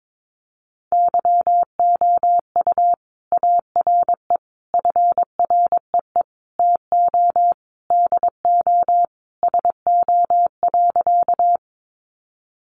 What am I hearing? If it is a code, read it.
Morse code, heard as a keyed tone, about 22 wpm, 700 Hz. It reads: YOU ARE FREE TO DO SO.
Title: Sentences from Top 500 Words 22wpm